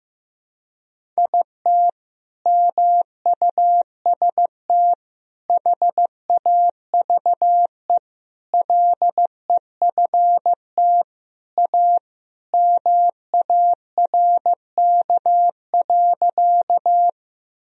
Morse code, heard as a keyed tone, about 15 wpm, 700 Hz. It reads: IT MUST HAVE LEFT A MARK.
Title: Sentences from Top 1000 Words - Review Part 1 - 15wpm